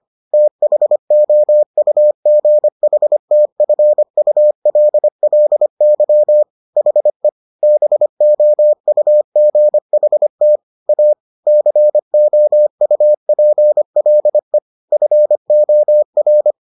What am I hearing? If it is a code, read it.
Morse code, heard as a keyed tone, about 25 wpm, 600 Hz. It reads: THOUGHTFULLY HE BOUGHT A COUPLE FOR